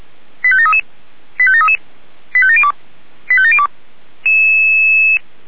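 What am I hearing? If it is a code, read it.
Five-tone selective call (ZVEI), heard as quick runs of five tones, tones of about 70 ms. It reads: 85733 85733 85702 85702
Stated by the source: ZVEI five-tone selective call received via SDR, address 89496, from the feuerwehr-orlamuende.org fire service audio archive.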